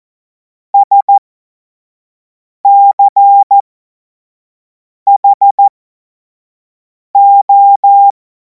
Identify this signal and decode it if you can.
Morse code, heard as a keyed tone, about 14 wpm, spaced as for 5 wpm, 800 Hz. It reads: SCHO
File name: Morsekurs / Witze / Witz 31. März 2026 25 F70